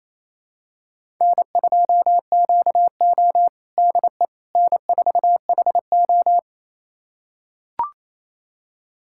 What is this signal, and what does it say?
Morse code, heard as a keyed tone, about 27 wpm, 700 Hz. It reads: N2QO DE N4HO
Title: USA Callsign DE Callsign - 4 Characters - 28wpm